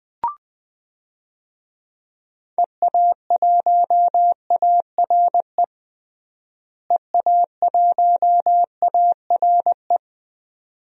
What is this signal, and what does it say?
Morse code, heard as a keyed tone, about 20 wpm, 700 Hz. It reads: EA1ARE EA1ARE
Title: DXCC-100 Call Signs - 6 Characters - Repeated 2x - 20wpm